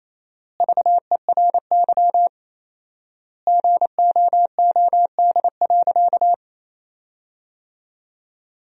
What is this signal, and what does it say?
Morse code, heard as a keyed tone, about 28 wpm, 700 Hz. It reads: VERY GOOD.
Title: 4x Word-Spacing - Sentences from Top 200 Words - 28wpm